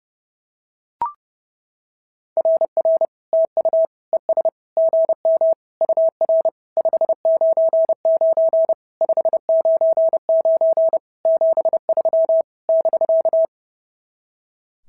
Morse code, heard as a keyed tone, about 28 wpm, 650 Hz. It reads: RR TU ES GM UR 599 599 73 <BK>
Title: SOTA QSO Practice 30wpm